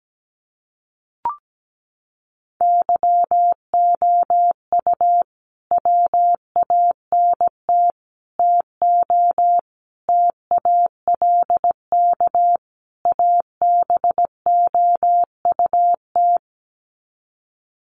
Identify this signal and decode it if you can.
Morse code, heard as a keyed tone, about 17 wpm, 700 Hz. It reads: YOU WANT TO TALK ABOUT
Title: Sets of 5 Words Encore 17wpm